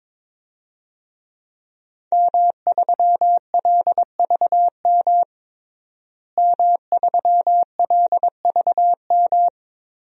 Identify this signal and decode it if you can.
Morse code, heard as a keyed tone, about 22 wpm, 700 Hz. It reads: M3LVM M3LVM
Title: DXCC-100 Call Signs - 5 Characters - Repeated 2x - 22wpm